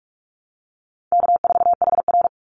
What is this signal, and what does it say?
Morse code, heard as a keyed tone, about 46 wpm, 700 Hz. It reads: K4HR
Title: Call Signs - 45wpm